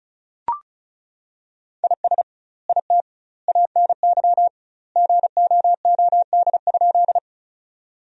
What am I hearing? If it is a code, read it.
Morse code, heard as a keyed tone, about 36 wpm, 700 Hz. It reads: IS IT ANY GOOD?
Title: 2x Word-Spacing - Sentences from Top 200 Words - 35wpm